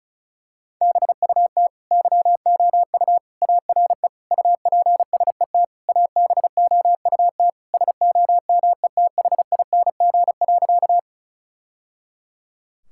Morse code, heard as a keyed tone, about 36 wpm, 700 Hz. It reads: BUT YOU ARE UPSET ABOUT SOMETHING.
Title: Sentences from Top 1000 Words - Review Part 5 - 35wpm